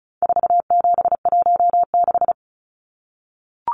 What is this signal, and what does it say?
Morse code, heard as a keyed tone, about 34 wpm, 700 Hz. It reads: 4716
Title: Numbers - 4-Digit 35wpm